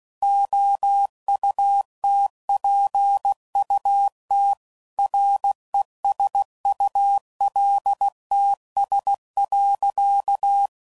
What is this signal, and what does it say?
Morse code, heard as a keyed tone, about 16 wpm, 800 Hz. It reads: OUTPUT RESULTS.
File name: CW Code Groups @15WPM / Technical Subjects / Gates Intro